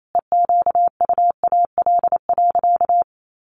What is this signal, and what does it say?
Morse code, heard as a keyed tone, about 28 wpm, 700 Hz. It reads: EQUAL.
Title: Sentences from Top 1500 Words 28wpm